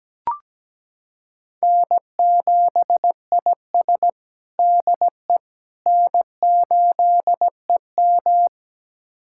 Morse code, heard as a keyed tone, about 17 wpm, 700 Hz. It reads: N7IS DE N8EM
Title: USA Callsign DE Callsign - 4 Characters - 17wpm